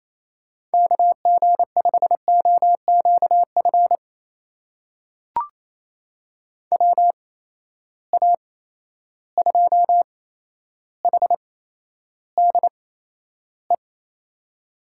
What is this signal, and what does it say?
Morse code, heard as a keyed tone, about 28 wpm, spaced as for 12 wpm, 700 Hz. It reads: KG5OQF WA2HDE
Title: USA Call Signs - 6 Characters - 8x Character Spacing - 28wpm